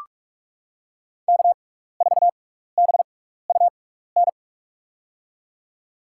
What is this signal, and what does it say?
Morse code, heard as a keyed tone, about 44 wpm, spaced as for 15 wpm, 690 Hz. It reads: K4BUN
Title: USA Call Signs - 6x Character Spacing - 45wpm